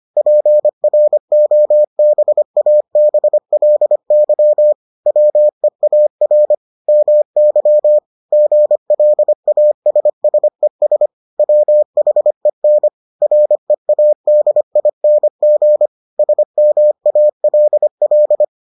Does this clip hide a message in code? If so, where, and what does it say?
Yes: Morse code from 0.2 to 18.4 s, heard as a keyed tone, about 25 wpm, 600 Hz. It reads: PROBABLY WEAR MY GLASSES WHEN READING SMALL